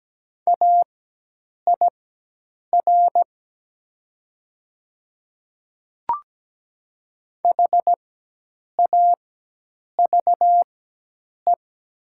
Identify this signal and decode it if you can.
Morse code, heard as a keyed tone, about 17 wpm, spaced as for 8 wpm, 700 Hz. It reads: AIR HAVE
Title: 4x Character Spacing - Top 300 Words - 17wpm